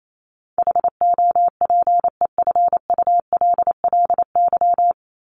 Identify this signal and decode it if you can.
Morse code, heard as a keyed tone, about 28 wpm, 700 Hz. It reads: HOPEFULLY
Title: Top 1251-1500 Words 28wpm